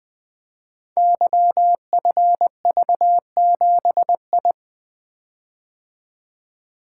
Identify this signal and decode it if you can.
Morse code, heard as a keyed tone, about 20 wpm, 700 Hz. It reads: YFV7I
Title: Code Groups - 5 Letter:Numbers - 20wpm